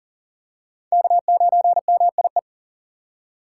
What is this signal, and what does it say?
Morse code, heard as a keyed tone, about 40 wpm, 700 Hz. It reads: K9MIE
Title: USA Call Signs - 5 Characters - 4x Character Spacing - 40wpm